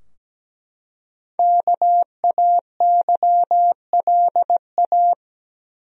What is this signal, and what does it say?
Morse code, heard as a keyed tone, about 17 wpm, 700 Hz. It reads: KAYLA
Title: USA Common Names - 17wpm